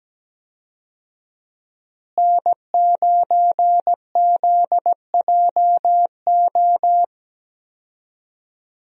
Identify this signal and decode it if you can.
Morse code, heard as a keyed tone, about 17 wpm, 700 Hz. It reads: N9ZJO
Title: USA Call Signs - 5 Characters - Repeated 3x - 17wpm